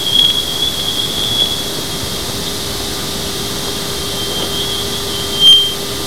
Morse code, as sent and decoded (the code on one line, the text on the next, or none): none